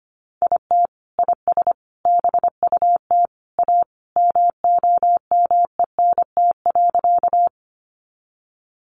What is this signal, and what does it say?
Morse code, heard as a keyed tone, about 25 wpm, 700 Hz. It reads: IT IS BUT A MOMENT.
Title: Sentences from Top 1000 Words 25wpm